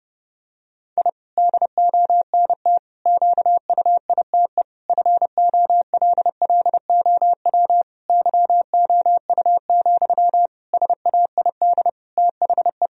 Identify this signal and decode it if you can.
Morse code, heard as a keyed tone, about 30 wpm, 700 Hz. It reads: I DONT QUITE FOLLOW YOU, SAID THE
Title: Sentences from Top 1000 Words - Review Part 4 - 30wpm